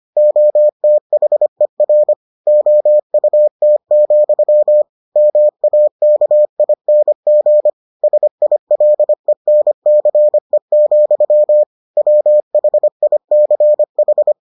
Morse code, heard as a keyed tone, about 25 wpm, 600 Hz. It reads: OTHER OUT, MAKING SILENCE, WHICH